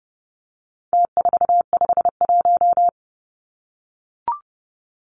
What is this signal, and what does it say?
Morse code, heard as a keyed tone, about 30 wpm, 700 Hz. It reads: T451
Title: Numbers - Contest Style - 4-Digit 30wpm